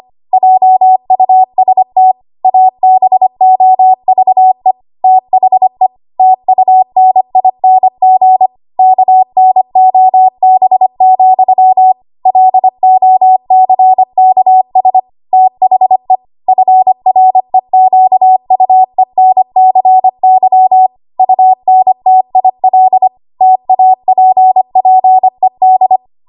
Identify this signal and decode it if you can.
Morse code, heard as a keyed tone, about 25 wpm, 750 Hz. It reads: JUST ABOVE THE TUNING KNOB, LOCKS THE FREQUENCY UNTIL TAPPED
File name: Archive / 25 WPM / 25WPM